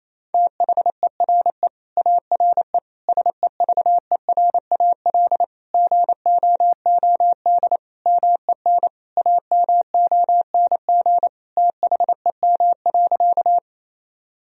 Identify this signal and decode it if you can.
Morse code, heard as a keyed tone, about 27 wpm, 700 Hz. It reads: THERE ARE SEVERAL GOOD MEN AMONG THEM.
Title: Sentences from Top 1000 Words - Review Part 2 - 28wpm